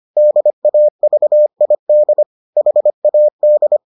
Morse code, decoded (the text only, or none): DAVID HAD